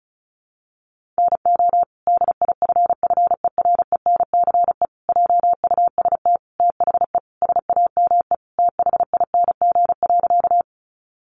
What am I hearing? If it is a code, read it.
Morse code, heard as a keyed tone, about 34 wpm, 700 Hz. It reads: NO DIFFERENCE JUST THE SAME THING.
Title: Sentences from Top 800 Words 35wpm